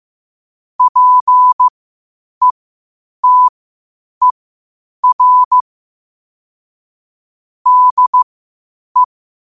Morse code, heard as a keyed tone, about 15 wpm, spaced as for 8 wpm, 1000 Hz. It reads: PETER DE
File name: qotdfiles / 8wpm